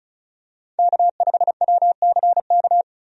Morse code, heard as a keyed tone, about 36 wpm, 700 Hz. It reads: K5WCK
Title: USA Call Signs - 5 Characters - 35wpm